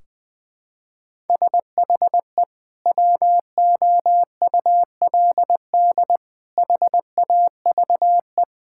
Morse code, heard as a keyed tone, about 19 wpm, 700 Hz. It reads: SHE WOULD HAVE
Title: Sentences from Top 1600 Words 20wpm